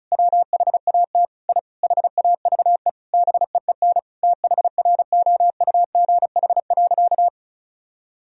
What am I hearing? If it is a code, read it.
Morse code, heard as a keyed tone, about 36 wpm, 700 Hz. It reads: WHAT I HAVE BEEN THROUGH.
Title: Sentences from Top 1000 Words - Review Part 1 - 35wpm